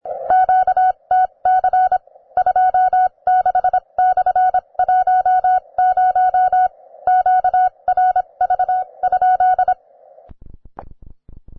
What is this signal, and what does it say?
Morse code, heard as a keyed tone, about 26 wpm, spaced as for 22 wpm, 740 Hz. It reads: QTC 26/10 QRV?